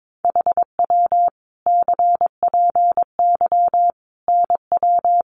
Morse code, heard as a keyed tone, about 22 wpm, 700 Hz. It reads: HW CPY NW
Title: Common QSO Phrases 22wpm